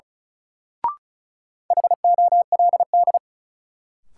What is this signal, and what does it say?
Morse code, heard as a keyed tone, about 36 wpm, 700 Hz. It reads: HOLD
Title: Sentences from Top 400 Words - New Word Drill - 35wpm